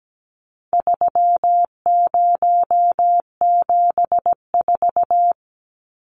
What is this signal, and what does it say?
Morse code, heard as a keyed tone, about 17 wpm, 700 Hz. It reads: 3074
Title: Numbers - 4-Digit 17wpm